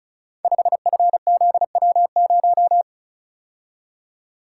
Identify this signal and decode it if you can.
Morse code, heard as a keyed tone, about 36 wpm, 700 Hz. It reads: 5FZW0
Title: Code Groups - 5 Letter:Numbers - 35wpm